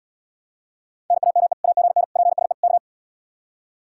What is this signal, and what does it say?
Morse code, heard as a keyed tone, about 79 wpm, 700 Hz. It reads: NAME AGN PSE K